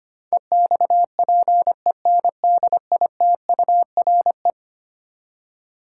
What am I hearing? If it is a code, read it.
Morse code, heard as a keyed tone, about 25 wpm, 700 Hz. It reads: EXPENDITURE
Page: morse-code-ninja-practice-25wpm